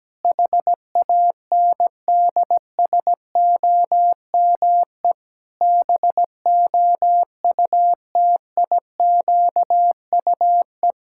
Morse code, heard as a keyed tone, about 17 wpm, 700 Hz. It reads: HANDSOME BOUTIQUE
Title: Pangram Challenge v2 - 17wpm